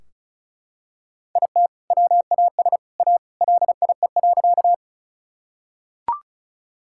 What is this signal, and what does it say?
Morse code, heard as a keyed tone, about 36 wpm, 700 Hz. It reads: IT WAS A LIE.
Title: Sentences from Top 1600 Words 35wpm